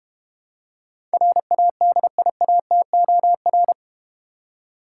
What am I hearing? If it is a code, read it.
Morse code, heard as a keyed tone, about 30 wpm, 700 Hz. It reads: RADIATOR
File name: practice / 25wpm / Difficult Words - Speed-Racing with Hints - 25wpm